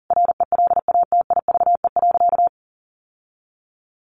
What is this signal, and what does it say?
Morse code, heard as a keyed tone, about 40 wpm, 700 Hz. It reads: RELATIVE.
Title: Sentences from Top 1400 Words 40wpm